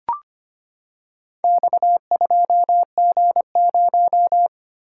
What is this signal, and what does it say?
Morse code, heard as a keyed tone, about 25 wpm, 700 Hz. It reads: X2G0
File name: practice / 25wpm / Four Letter-Number Combinations - 25wpm